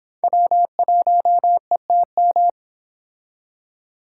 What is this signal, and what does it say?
Morse code, heard as a keyed tone, about 31 wpm, 700 Hz. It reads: W1ETM